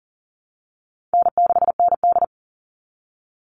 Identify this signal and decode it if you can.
Morse code, heard as a keyed tone, about 40 wpm, 700 Hz. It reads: N6ND
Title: All Cut Numbers - 4-Digit 40wpm